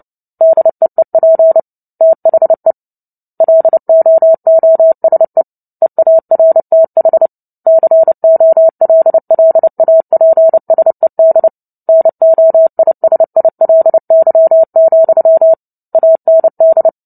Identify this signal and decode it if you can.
Morse code, heard as a keyed tone, about 30 wpm, 650 Hz. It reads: DEEP THE LOOSE EARTH COLLAPSED NOISILY, AND
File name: Book Two / 03-Chapter 3 - 29 WPM